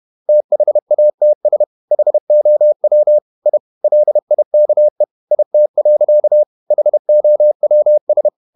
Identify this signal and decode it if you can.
Morse code, heard as a keyed tone, about 31 wpm, 600 Hz. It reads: THATS HOW I LIKE IT. HOWS
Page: Never 31wpm | CW med Gnister